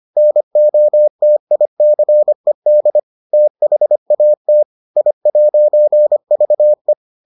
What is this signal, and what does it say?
Morse code, heard as a keyed tone, about 25 wpm, 600 Hz. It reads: NOTICED THAT I'VE